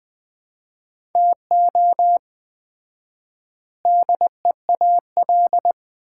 Morse code, heard as a keyed tone, about 21 wpm, 700 Hz. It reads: TO DEAL
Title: 4x Word-Spacing - Sets of 3 Words - 20wpm